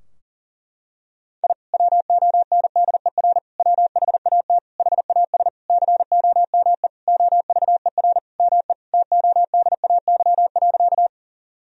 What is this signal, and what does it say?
Morse code, heard as a keyed tone, about 39 wpm, 700 Hz. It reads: I WONDER WHAT HAS COME OVER ME TODAY.
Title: Sentences from Top 1000 Words - Review Part 4 - 40wpm